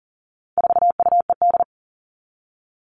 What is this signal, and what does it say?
Morse code, heard as a keyed tone, about 40 wpm, 700 Hz. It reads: 4UED
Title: All Cut Numbers - 4-Digit 40wpm